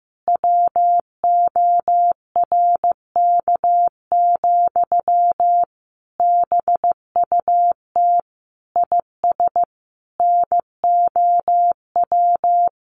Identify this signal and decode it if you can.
Morse code, heard as a keyed tone, about 15 wpm, 700 Hz. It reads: WORK, BUT IS NOW
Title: Sentences from Top 1500 Words 15wpm